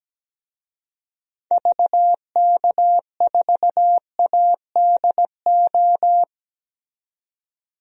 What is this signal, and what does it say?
Morse code, heard as a keyed tone, about 17 wpm, 700 Hz. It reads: VK4ADO
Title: DXCC-100 Call Signs - 6 Characters - 17wpm